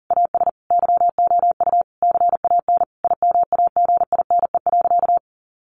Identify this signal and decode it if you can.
Morse code, heard as a keyed tone, about 40 wpm, 700 Hz. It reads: AS YOU CAN IMAGINE.
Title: Sentences from Top 1400 Words 40wpm